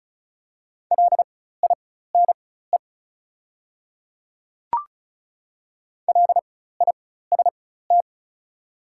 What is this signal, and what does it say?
Morse code, heard as a keyed tone, about 35 wpm, spaced as for 16 wpm, 690 Hz. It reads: LINE LIST
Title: 4x Character Spacing - Top 500 Words - 35wpm